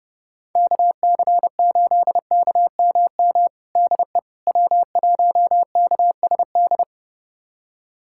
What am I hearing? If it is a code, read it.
Morse code, heard as a keyed tone, about 32 wpm, 700 Hz. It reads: KC8KMM DE W1KSD
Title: USA Callsign DE Callsign 30wpm